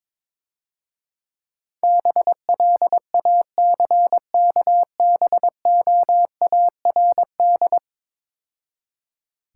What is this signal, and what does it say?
Morse code, heard as a keyed tone, about 23 wpm, 690 Hz. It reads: BLACKBOARD
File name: Easy Words - Speed-Racing with Hints - 22wpm